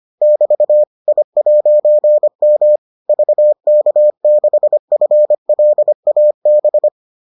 Morse code, heard as a keyed tone, about 25 wpm, 600 Hz. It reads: <BT> I'M VK6FLAB